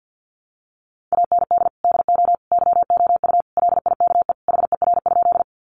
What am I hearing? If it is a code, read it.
Morse code, heard as a keyed tone, about 50 wpm, 700 Hz. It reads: AND DO YOU LIKE HER?
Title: Sentences from Top 100 Words 50wpm